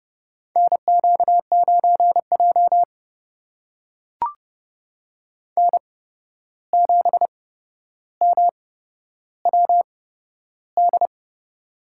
Morse code, heard as a keyed tone, about 30 wpm, spaced as for 12 wpm, 700 Hz. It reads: NQ9J N7MWD
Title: USA Call Signs - 8x Character Spacing - 30wpm